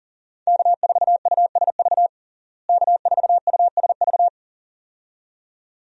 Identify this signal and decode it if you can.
Morse code, heard as a keyed tone, about 40 wpm, 700 Hz. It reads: K4USV K4USV
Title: USA Call Signs - 5 Characters - Repeated 2x - 40wpm